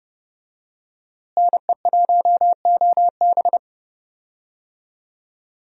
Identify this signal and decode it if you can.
Morse code, heard as a keyed tone, about 30 wpm, 700 Hz. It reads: NE1OB
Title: USA Call Signs - 5 Characters - Repeated 2x - 30wpm